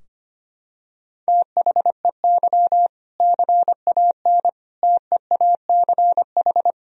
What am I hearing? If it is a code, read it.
Morse code, heard as a keyed tone, about 25 wpm, 700 Hz. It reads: THEY CAN TEACH
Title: Sentences from Top 1000 Words - Review Part 2 - 25wpm